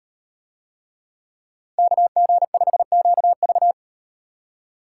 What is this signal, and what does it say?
Morse code, heard as a keyed tone, about 38 wpm, 700 Hz. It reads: KG5QV